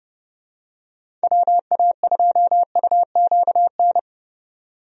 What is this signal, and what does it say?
Morse code, heard as a keyed tone, about 30 wpm, 700 Hz. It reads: WA2UQN
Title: USA Call Signs - 6 Characters - 30wpm